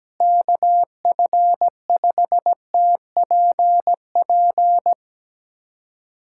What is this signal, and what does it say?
Morse code, heard as a keyed tone, about 17 wpm, 700 Hz. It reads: KF5TPP
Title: USA Callsign DE Callsign - 6 Characters - 17wpm